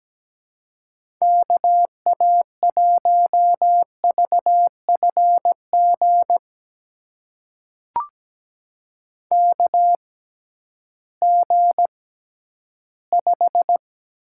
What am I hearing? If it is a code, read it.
Morse code, heard as a keyed tone, about 17 wpm, spaced as for 11 wpm, 700 Hz. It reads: KA1VFG KG5
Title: USA Call Signs - 6 Characters - 6x Character Spacing - 17wpm